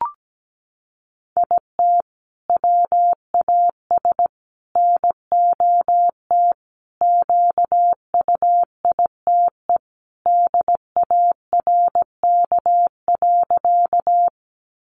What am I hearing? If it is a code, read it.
Morse code, heard as a keyed tone, about 17 wpm, 700 Hz. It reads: IT WAS NOT QUITE DARK.
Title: Sentences from Top 700 Words 17wpm